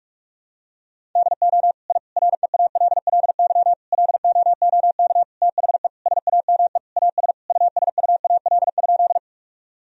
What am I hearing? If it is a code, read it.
Morse code, heard as a keyed tone, about 41 wpm, 700 Hz. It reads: DO I REALLY LOOK THE SAME AS USUAL?
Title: Sentences from Top 1000 Words - Review Part 2 - 45wpm